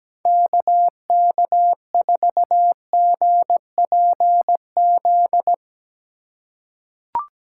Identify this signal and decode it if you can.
Morse code, heard as a keyed tone, about 17 wpm, 700 Hz. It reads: KK4GPZ
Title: USA Call Signs - 6 Characters - 4x Character Spacing - 17wpm